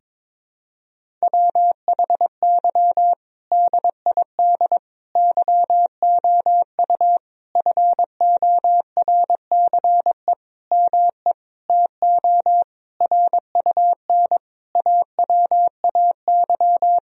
Morse code, heard as a keyed tone, about 22 wpm, 700 Hz. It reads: WHY DID YOU FORCE ME TO RUN AWAY
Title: Sentences from Top 1000 Words - Review Part 3 - 22wpm